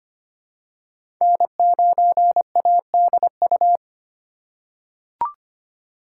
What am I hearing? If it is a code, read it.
Morse code, heard as a keyed tone, about 25 wpm, 700 Hz. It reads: N9ADU
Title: USA Call Signs - 5 Characters - 3x Character Spacing - 25wpm